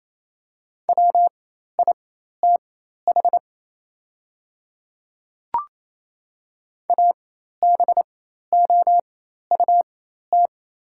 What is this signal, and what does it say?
Morse code, heard as a keyed tone, about 28 wpm, spaced as for 13 wpm, 700 Hz. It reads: WITH ABOUT
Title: 4x Character Spacing - Top 100 Words - 28wpm